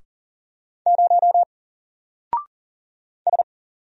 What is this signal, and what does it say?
Morse code, heard as a keyed tone, about 40 wpm, 700 Hz. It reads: 0 S
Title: Single Character (Letter:Number:Punctuation:Prosign) 40wpm